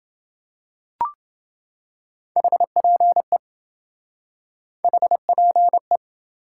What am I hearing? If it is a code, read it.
Morse code, heard as a keyed tone, about 27 wpm, 700 Hz. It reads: HPE HPE